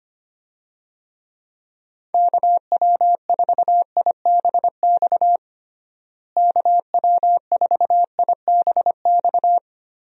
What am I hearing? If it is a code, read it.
Morse code, heard as a keyed tone, about 25 wpm, 700 Hz. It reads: KW4IBX KW4IBX
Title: USA Call Signs - 6 Characters - Repeated 3x - 25wpm